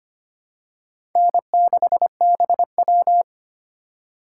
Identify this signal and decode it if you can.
Morse code, heard as a keyed tone, about 25 wpm, 700 Hz. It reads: N6BW
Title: USA Call Signs - 4 Characters - 4x Character Spacing - 25wpm